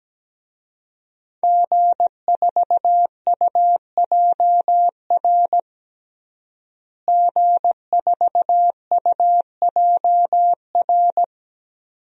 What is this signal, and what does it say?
Morse code, heard as a keyed tone, about 17 wpm, 700 Hz. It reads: G4UJR G4UJR
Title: DXCC-100 Call Signs - 5 Characters - Repeated 2x - 17wpm